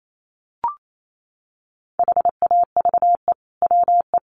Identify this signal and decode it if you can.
Morse code, heard as a keyed tone, about 28 wpm, 700 Hz. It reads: HAVE WE
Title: Sentences from Top 1400 Words 28wpm